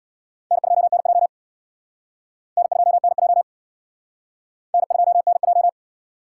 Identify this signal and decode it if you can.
Morse code, heard as a keyed tone, about 61 wpm, 700 Hz. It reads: N2NJ N2NJ N2NJ
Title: Speed Racing - USA 1x2 Call Signs 50wpm